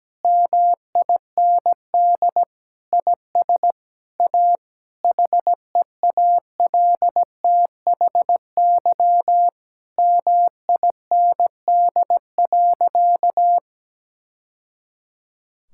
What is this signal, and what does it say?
Morse code, heard as a keyed tone, about 17 wpm, 700 Hz. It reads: MIND IS A HEALTHY MIND.
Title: Sentences from Top 1000 Words - Review Part 5 - 17wpm